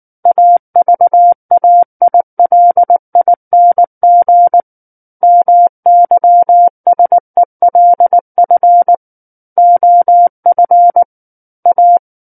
Morse code, heard as a keyed tone, about 19 wpm, 700 Hz. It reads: AVAILING MYSELF OF A
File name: Book One / 10-Chapter 10 - 19 WPM